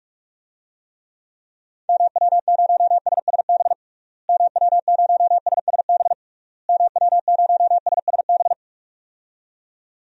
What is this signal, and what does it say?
Morse code, heard as a keyed tone, about 46 wpm, 700 Hz. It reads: MW0SSB MW0SSB MW0SSB
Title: DXCC-100 Call Signs - 6 Characters - Repeated 3x - 45wpm